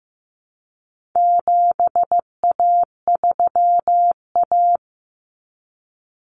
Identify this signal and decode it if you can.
Morse code, heard as a keyed tone, about 15 wpm, 700 Hz. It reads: 7A3A
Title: Common Cut Numbers - 4-Digit 15wpm